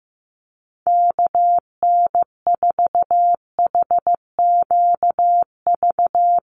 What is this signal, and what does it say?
Morse code, heard as a keyed tone, about 15 wpm, 700 Hz. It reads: KN4HQV
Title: USA Call Signs 15wpm